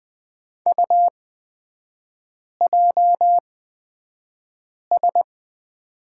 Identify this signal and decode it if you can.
Morse code, heard as a keyed tone, about 20 wpm, spaced as for 5 wpm, 700 Hz. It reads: UJS
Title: 255 Lesson 41e - Period - Three Characters 20wpm